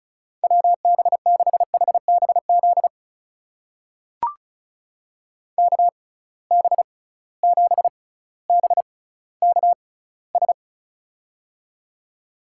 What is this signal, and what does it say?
Morse code, heard as a keyed tone, about 35 wpm, spaced as for 18 wpm, 700 Hz. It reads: WB6HBZ KB7BKS
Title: USA Call Signs - 6 Characters - 6x Character Spacing - 35wpm